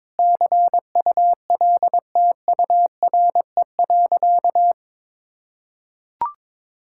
Morse code, heard as a keyed tone, about 22 wpm, 700 Hz. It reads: CULTURE.
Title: Sentences from Top 1000 Words - Review Part 6 - 22wpm